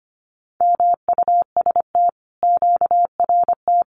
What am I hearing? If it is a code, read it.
Morse code, heard as a keyed tone, about 25 wpm, 700 Hz. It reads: MUST QRT
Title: Common QSO Phrases Encore 25wpm